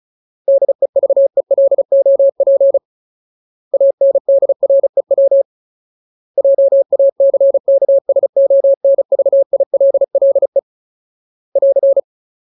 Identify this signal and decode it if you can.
Morse code, heard as a keyed tone, about 34 wpm, 550 Hz. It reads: DEVELOP ANDREW JACKSONVILLE <AR>
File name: Ultimate Word List / 35WPM Ultimate